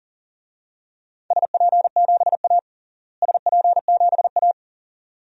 Morse code, heard as a keyed tone, about 40 wpm, 700 Hz. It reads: SP7A SP7A
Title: DXCC-100 Call Signs - 4 Characters - Repeated 2x - 40wpm